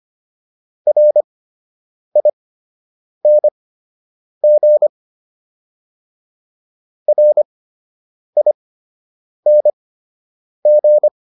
Morse code, heard as a keyed tone, about 25 wpm, spaced as for 8 wpm, 600 Hz. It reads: RING RING